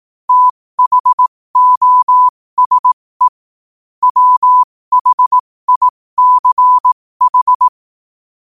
Morse code, heard as a keyed tone, about 18 wpm, spaced as for 15 wpm, 1000 Hz. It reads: THOSE WHICH
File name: qotdfiles / 15wpm